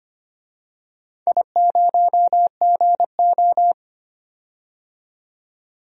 Morse code, heard as a keyed tone, about 25 wpm, 700 Hz. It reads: I0GO
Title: DXCC-100 Call Signs - 4 Characters - 25wpm